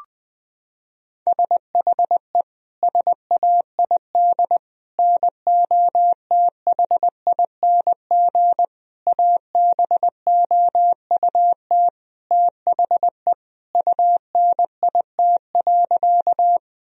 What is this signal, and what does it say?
Morse code, heard as a keyed tone, about 20 wpm, 700 Hz. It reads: SHE SAID NOTHING ABOUT THE UNIT.
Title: Sentences from Top 1000 Words - Review Part 2 - 20wpm